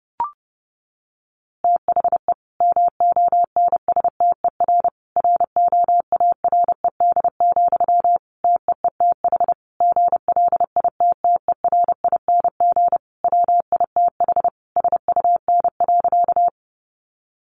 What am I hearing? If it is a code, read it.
Morse code, heard as a keyed tone, about 30 wpm, 700 Hz. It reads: THE MONSTER ROARED, TEETH GLITTERING WITH SUN.
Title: A Sound Of Thunder 30wpm